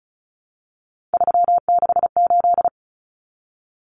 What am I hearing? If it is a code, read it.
Morse code, heard as a keyed tone, about 34 wpm, 700 Hz. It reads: 368
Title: Numbers - 3-Digit 35wpm